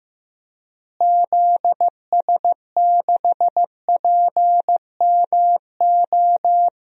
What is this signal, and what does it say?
Morse code, heard as a keyed tone, about 15 wpm, 700 Hz. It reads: ZS6PMO
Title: DXCC-100 Call Signs - 6 Characters - Repeated 2x - 15wpm